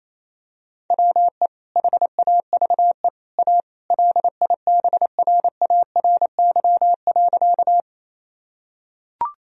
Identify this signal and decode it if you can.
Morse code, heard as a keyed tone, about 27 wpm, 700 Hz. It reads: WE HAVE A LIBRARY.
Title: Sentences from Top 1000 Words - Review Part 4 - 28wpm